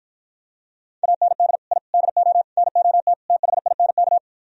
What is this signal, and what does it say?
Morse code, heard as a keyed tone, about 52 wpm, 700 Hz. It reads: AND I DO NOT THINK